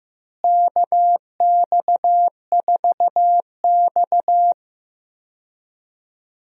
Morse code, heard as a keyed tone, about 15 wpm, 700 Hz. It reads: KX4X
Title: USA Call Signs - 4 Characters - Repeated 3x - 15wpm